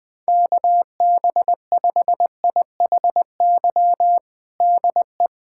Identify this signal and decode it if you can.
Morse code, heard as a keyed tone, about 20 wpm, 700 Hz. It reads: KB5IHY DE
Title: USA Callsign DE Callsign - 6 Characters - 20wpm